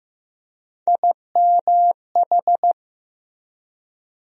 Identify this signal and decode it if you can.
Morse code, heard as a keyed tone, about 15 wpm, 700 Hz. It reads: IMH
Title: USA Call Sign Postfixes - 3 Letters 15wpm